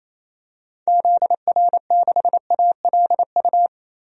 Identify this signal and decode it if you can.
Morse code, heard as a keyed tone, about 28 wpm, 700 Hz. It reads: ZR6ALU
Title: DXCC-100 Call Signs - 6 Characters - Repeated 3x - 28wpm